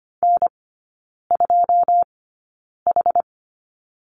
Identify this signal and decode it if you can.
Morse code, heard as a keyed tone, about 25 wpm, spaced as for 9 wpm, 690 Hz. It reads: N2H
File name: Lesson 14e - Number 2 - Three Characters 25wpm